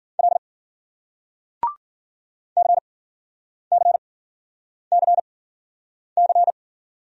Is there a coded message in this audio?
Yes: Morse code from 0.2 to 6.5 s, heard as a keyed tone, about 53 wpm, spaced as for 9 wpm, 700 Hz. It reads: L CCCC